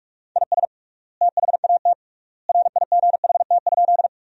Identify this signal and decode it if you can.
Morse code, heard as a keyed tone, about 45 wpm, 700 Hz. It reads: IS THAT RIGHT?
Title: 3x Word-Spacing - Sentences from Top 200 Words - 45wpm